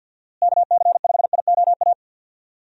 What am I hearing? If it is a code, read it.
Morse code, heard as a keyed tone, about 50 wpm, 700 Hz. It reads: KC5IOA
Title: USA Call Signs - 6 Characters - Repeated 3x - 50wpm